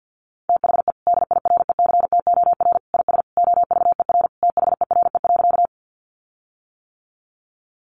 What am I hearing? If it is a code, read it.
Morse code, heard as a keyed tone, about 50 wpm, 700 Hz. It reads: THE DIRECTOR IS OVER THERE.
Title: Sentences from Top 1300 Words 50wpm